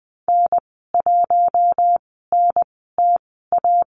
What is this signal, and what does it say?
Morse code, heard as a keyed tone, about 20 wpm, spaced as for 14 wpm, 700 Hz. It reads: N1NTA
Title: 076 Lesson 13g - Letter L - Call Signs 20wpm